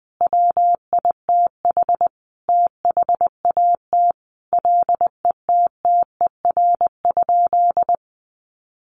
Morse code, heard as a keyed tone, about 20 wpm, 700 Hz. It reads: WITH THAT LETTER?
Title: Sentences from Top 300 Words 20wpm